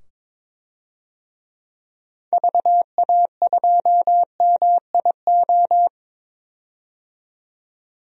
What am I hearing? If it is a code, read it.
Morse code, heard as a keyed tone, about 22 wpm, 700 Hz. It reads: VA2MIO
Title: DXCC-100 Call Signs - 6 Characters - 22wpm